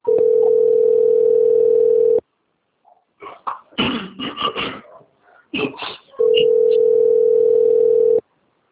Ringback tone (North America), 440 + 480 Hz.
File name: out- - - - - .13067